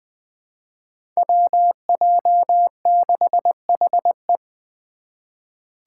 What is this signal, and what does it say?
Morse code, heard as a keyed tone, about 20 wpm, 700 Hz. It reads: WJ6HE
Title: Five Challenging Letter-Number Combinations 20wpm